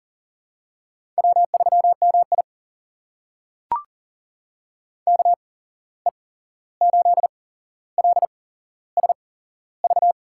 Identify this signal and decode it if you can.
Morse code, heard as a keyed tone, about 40 wpm, spaced as for 15 wpm, 700 Hz. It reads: W3MI KE8LSV
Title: USA Call Signs - 8x Character Spacing - 40wpm